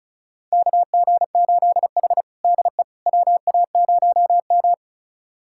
Morse code, heard as a keyed tone, about 36 wpm, 700 Hz. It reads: KG8H DE WA0M
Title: USA Callsign DE Callsign - 4 Characters - 35wpm